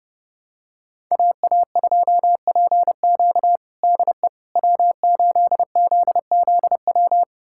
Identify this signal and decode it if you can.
Morse code, heard as a keyed tone, about 30 wpm, 700 Hz. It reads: AA2PQ DE W8ZZW
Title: USA Callsign DE Callsign 30wpm